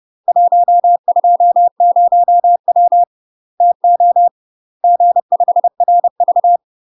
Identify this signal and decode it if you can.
Morse code, heard as a keyed tone, about 30 wpm, 700 Hz. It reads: 120W TO G5RV
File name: UK MASTER QSOs TRAINING / 30WPM / UK MASTER QSOs 0019